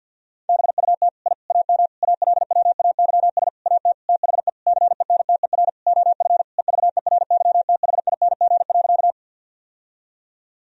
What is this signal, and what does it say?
Morse code, heard as a keyed tone, about 51 wpm, 700 Hz. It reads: BUT I AM ALWAYS AT THE CENTER OF EVERYTHING.
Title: Sentences from Top 1000 Words - Review Part 2 - 50wpm